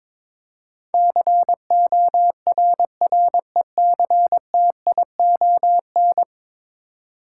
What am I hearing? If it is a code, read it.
Morse code, heard as a keyed tone, about 22 wpm, 700 Hz. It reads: CORRECTION
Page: morse-code-ninja-practice-22wpm